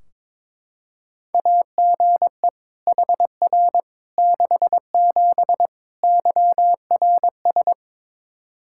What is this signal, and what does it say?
Morse code, heard as a keyed tone, about 22 wpm, 700 Hz. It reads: AGE HR 67 YRS
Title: K1USN SST QSO Practice 22wpm